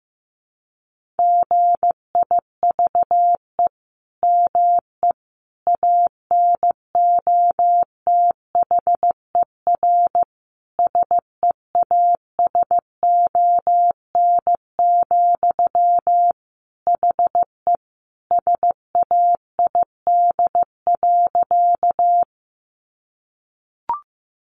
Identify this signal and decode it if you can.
Morse code, heard as a keyed tone, about 15 wpm, 700 Hz. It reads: GIVE ME ANOTHER SEASON, HE SAID.
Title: Sentences from Top 1400 Words 15wpm